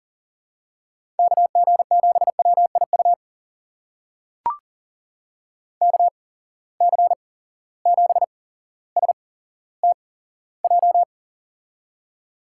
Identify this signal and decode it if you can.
Morse code, heard as a keyed tone, about 40 wpm, spaced as for 17 wpm, 700 Hz. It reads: KG7WIU KC7STJ
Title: USA Call Signs - 6 Characters - 8x Character Spacing - 40wpm